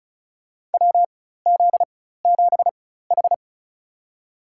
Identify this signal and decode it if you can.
Morse code, heard as a keyed tone, about 34 wpm, spaced as for 16 wpm, 700 Hz. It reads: WZ7H
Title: USA Call Signs - 4 Characters - 4x Character Spacing - 35wpm